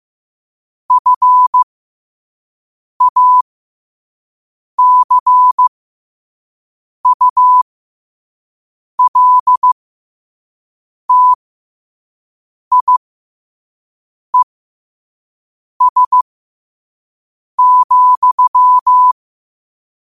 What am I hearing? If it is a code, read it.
Morse code, heard as a keyed tone, about 15 wpm, spaced as for 5 wpm, 1000 Hz. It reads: FACULTIES,
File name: qotdfiles / 5wpm